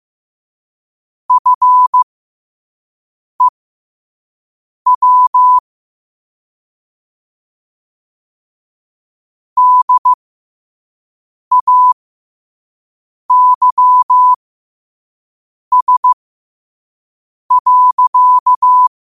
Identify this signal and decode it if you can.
Morse code, heard as a keyed tone, about 15 wpm, spaced as for 5 wpm, 1000 Hz. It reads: FEW DAYS.
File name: qotdfiles / 5wpm